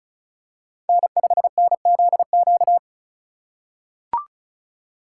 Morse code, heard as a keyed tone, about 36 wpm, 700 Hz. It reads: N5NZQ
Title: USA Call Signs - 3x Character Spacing - 35wpm